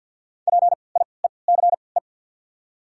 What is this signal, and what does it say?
Morse code, heard as a keyed tone, about 53 wpm, spaced as for 29 wpm, 700 Hz. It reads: PIECE
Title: 3x Character Spacing - Top 500 Words - 50wpm